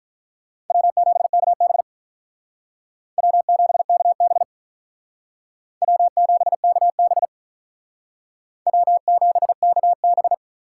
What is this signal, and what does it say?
Morse code, heard as a keyed tone, about 43 wpm, 700 Hz. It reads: W7KB W7KB W7KB W7KB
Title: Speed Racing - USA 1x2 Call Signs 35wpm